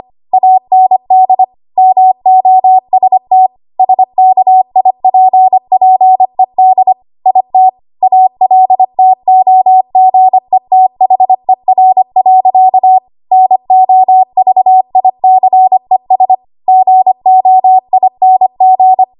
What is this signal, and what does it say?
Morse code, heard as a keyed tone, about 25 wpm, 750 Hz. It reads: AND MOST SKIPPED IT ALTOGETHER. NOVICES GOING
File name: Archive / 25 WPM / 25WPM